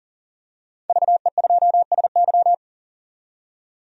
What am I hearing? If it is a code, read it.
Morse code, heard as a keyed tone, about 40 wpm, 700 Hz. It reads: VE2SY
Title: DXCC-100 Call Signs - 5 Characters - 40wpm